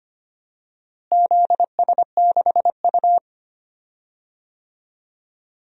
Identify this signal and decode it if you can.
Morse code, heard as a keyed tone, about 25 wpm, 700 Hz. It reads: ZS6U
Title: DXCC-100 Call Signs - 4 Characters - Repeated 3x - 25wpm